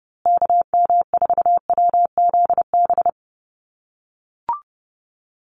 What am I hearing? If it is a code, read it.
Morse code, heard as a keyed tone, about 29 wpm, 700 Hz. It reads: KM4WZB
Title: USA Call Signs Repeated 3x 30wpm